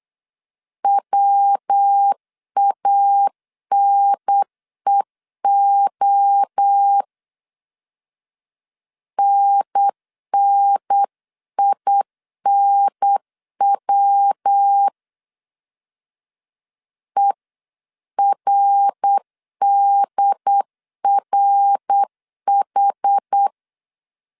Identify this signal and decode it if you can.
Morse code, heard as a keyed tone, about 8 wpm, 800 Hz. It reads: WANEO NNINW ERDRH